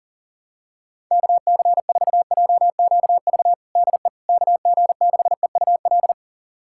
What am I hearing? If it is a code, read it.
Morse code, heard as a keyed tone, about 40 wpm, 700 Hz. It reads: KC4JQV DE KG6EUL
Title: USA Callsign DE Callsign - 6 Characters - 40wpm